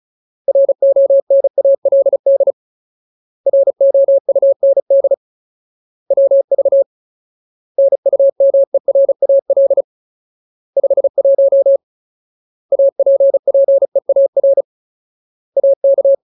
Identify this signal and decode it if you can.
Morse code, heard as a keyed tone, about 34 wpm, 550 Hz. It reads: RONALD ROUND WV NUMERAL 51 APPEAR AK
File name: Ultimate Word List / 35WPM Ultimate